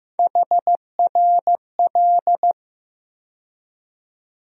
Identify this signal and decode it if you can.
Morse code, heard as a keyed tone, about 15 wpm, 700 Hz. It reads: HRL
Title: USA Call Sign Postfixes - 3 Letters 15wpm